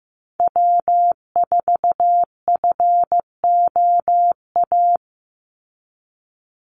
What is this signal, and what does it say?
Morse code, heard as a keyed tone, about 15 wpm, 700 Hz. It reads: W4FOA
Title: Call Signs Repeated 3x - 15wpm